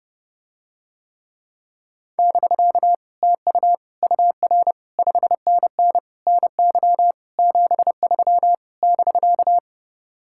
Morse code, heard as a keyed tone, about 32 wpm, 700 Hz. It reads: <BK> TU UR 5NN NY 73 <BK>